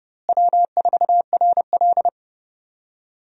Morse code, heard as a keyed tone, about 30 wpm, 700 Hz. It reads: W4RL
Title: USA Call Signs - 4 Characters - Repeated 2x - 30wpm